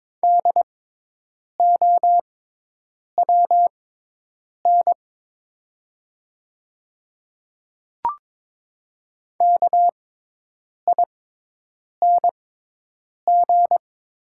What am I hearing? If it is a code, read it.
Morse code, heard as a keyed tone, about 23 wpm, spaced as for 8 wpm, 700 Hz. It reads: DOWN KING
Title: 6x Character Spacing - Top 500 Words - 22wpm